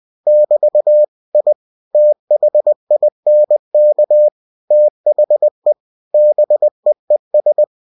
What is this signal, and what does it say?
Morse code, heard as a keyed tone, about 20 wpm, 600 Hz. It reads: <BT> I THINK THE BEES
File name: cw-books / winnie-the-pooh / winnie-20wpm-page-0007